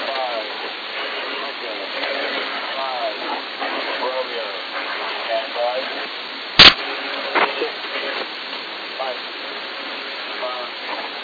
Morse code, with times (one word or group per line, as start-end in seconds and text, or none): none